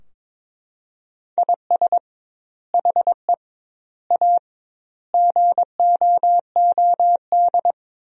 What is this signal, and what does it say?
Morse code, heard as a keyed tone, about 22 wpm, 700 Hz. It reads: IS HE A GOOD
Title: 2x Word-Spacing - Sentences from Top 200 Words - 22wpm